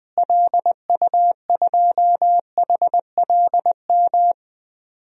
Morse code, heard as a keyed tone, about 20 wpm, 700 Hz. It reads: LU2HLM
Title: DXCC-100 Call Signs - 6 Characters - Repeated 3x - 20wpm